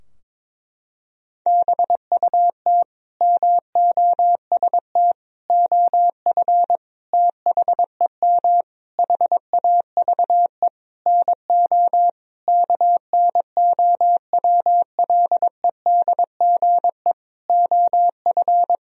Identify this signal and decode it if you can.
Morse code, heard as a keyed tone, about 21 wpm, 700 Hz. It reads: BUT MOST OF THEM HAVE NO KNOWLEDGE OF
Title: Sentences from Top 1000 Words - Review Part 3 - 22wpm